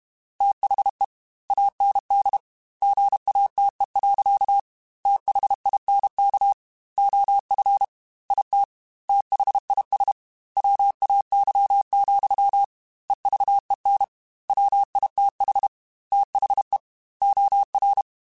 Morse code, heard as a keyed tone, about 32 wpm, 800 Hz. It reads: THE AND GATE. THINK OF IT THIS WAY, EVEN WITH THE OR
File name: Technical Subjects / Technical Subjects-mp3s / OR Gate @30